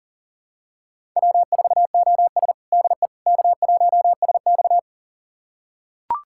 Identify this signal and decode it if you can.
Morse code, heard as a keyed tone, about 37 wpm, 700 Hz. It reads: W4OS DE K1SX
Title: USA Callsign DE Callsign - 4 Characters - 40wpm